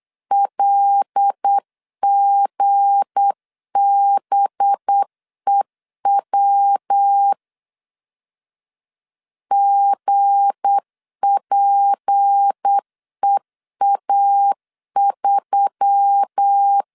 Morse code, heard as a keyed tone, about 8 wpm, 800 Hz. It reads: LGBEW GPEA3
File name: LEZIONI / 7 GF9Z / 7 GF9Z 8wpm 3213